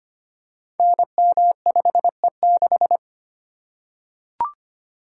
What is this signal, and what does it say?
Morse code, heard as a keyed tone, about 25 wpm, 700 Hz. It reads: NM5E6
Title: Code Groups - 5 Letter:Numbers - 25wpm